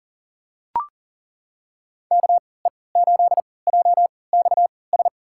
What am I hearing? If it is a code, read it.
Morse code, heard as a keyed tone, about 40 wpm, spaced as for 23 wpm, 700 Hz. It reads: KE8JXS
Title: USA Call Signs - 6 Characters - 3x Character Spacing - 40wpm